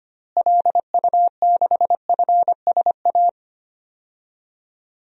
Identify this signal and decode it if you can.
Morse code, heard as a keyed tone, about 25 wpm, 700 Hz. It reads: LU6FSA
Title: DXCC-100 Call Signs - 6 Characters - Repeated 3x - 25wpm